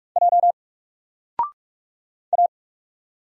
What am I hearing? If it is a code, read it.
Morse code, heard as a keyed tone, about 44 wpm, 700 Hz. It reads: J A
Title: Single Letter-Number 45wpm